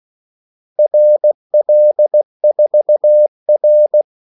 Morse code, heard as a keyed tone, about 16 wpm, 600 Hz. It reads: RL4R
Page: Kaldesignaler 16 wpm